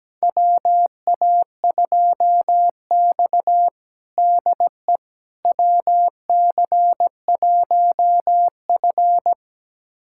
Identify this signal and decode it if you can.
Morse code, heard as a keyed tone, about 17 wpm, 700 Hz. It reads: WA2X DE WC1F
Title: USA Callsign DE Callsign - 4 Characters - 17wpm